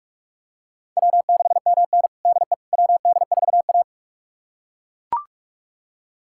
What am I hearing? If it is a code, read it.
Morse code, heard as a keyed tone, about 42 wpm, 700 Hz. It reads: W6MN DE WD4A
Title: USA Callsign DE Callsign - 4 Characters - 45wpm